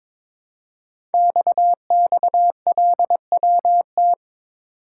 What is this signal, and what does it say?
Morse code, heard as a keyed tone, about 22 wpm, 700 Hz. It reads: XXLWT
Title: Code Groups - 5 Letters - 22wpm